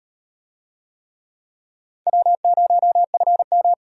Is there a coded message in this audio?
Yes: Morse code from 2.1 to 3.7 s, heard as a keyed tone, about 36 wpm, 700 Hz. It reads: W0FM